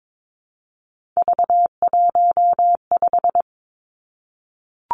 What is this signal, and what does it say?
Morse code, heard as a keyed tone, about 22 wpm, 700 Hz. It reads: V15
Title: Three Challenging Letter-Number Combinations 22wpm